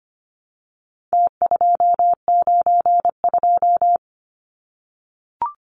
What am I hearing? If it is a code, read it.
Morse code, heard as a keyed tone, about 25 wpm, 700 Hz. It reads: T292
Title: Numbers - Contest Style - 4-Digit 25wpm